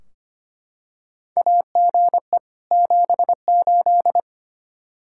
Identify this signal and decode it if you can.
Morse code, heard as a keyed tone, about 25 wpm, 700 Hz. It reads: AGE 78
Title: Practice QSO - Exchange 3 - More Info - 25wpm